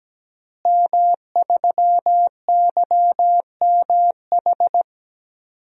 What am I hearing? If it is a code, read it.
Morse code, heard as a keyed tone, about 17 wpm, 700 Hz. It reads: M3YMH
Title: DXCC-100 Call Signs - 5 Characters - Repeated 2x - 17wpm